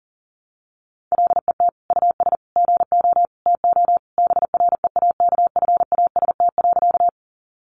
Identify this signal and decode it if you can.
Morse code, heard as a keyed tone, about 40 wpm, 700 Hz. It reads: LET US GO TO BREAKFAST.
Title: Sentences from Top 1500 Words 40wpm